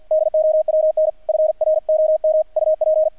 Morse code, heard as a keyed tone, about 48 wpm, 620 Hz, with radio noise. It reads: DOWN FROM UP
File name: archived / CRSnet